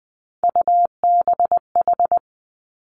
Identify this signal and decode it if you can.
Morse code, heard as a keyed tone, about 20 wpm, 700 Hz. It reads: UBH
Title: Three Challenging Letter-Number Combinations 20wpm